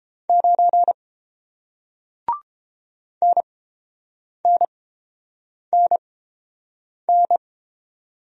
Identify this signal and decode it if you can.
Morse code, heard as a keyed tone, about 30 wpm, spaced as for 7 wpm, 700 Hz. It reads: 9 NNNN